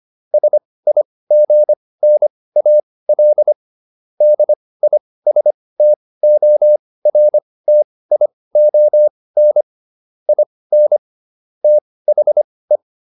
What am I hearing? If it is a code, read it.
Morse code, heard as a keyed tone, about 25 wpm, spaced as for 18 wpm, 600 Hz. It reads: SIGNAL DISTORTION IN THE